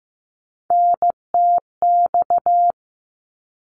Morse code, heard as a keyed tone, about 15 wpm, 700 Hz. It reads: NTX
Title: ARRL Sections 15wpm